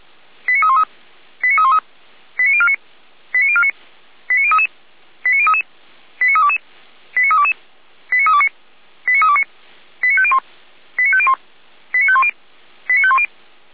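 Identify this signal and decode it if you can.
Five-tone selective call (ZVEI), heard as quick runs of five tones, tones of about 70 ms. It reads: 89313 89313 89049 89049 89033 89033 89322 89322 89329 89329 89501 89501 89510 89510
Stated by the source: ZVEI five-tone selective call 89305 received by SDR, recorded on the feuerwehr-orlamuende fire brigade server.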